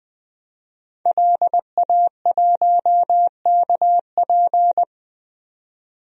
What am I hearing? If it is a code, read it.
Morse code, heard as a keyed tone, about 21 wpm, 700 Hz. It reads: LA1KP
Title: DXCC-100 Call Signs - 5 Characters - 20wpm